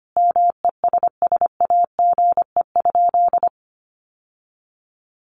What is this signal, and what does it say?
Morse code, heard as a keyed tone, about 25 wpm, 700 Hz. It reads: MESSAGE?
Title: Sentences from Top 1100 Words 25wpm